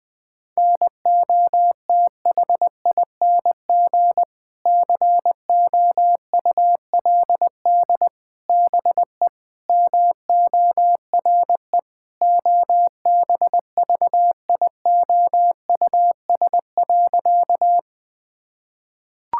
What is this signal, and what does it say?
Morse code, heard as a keyed tone, about 21 wpm, 700 Hz. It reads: NOTHING COULD BE MORE OBVIOUS.
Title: Sentences from Top 1600 Words 20wpm